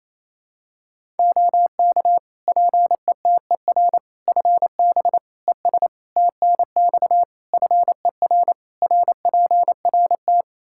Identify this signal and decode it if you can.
Morse code, heard as a keyed tone, about 28 wpm, 700 Hz. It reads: OK PETER FB ES TNX FER RPRT
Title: K1USN SST QSO Practice 28wpm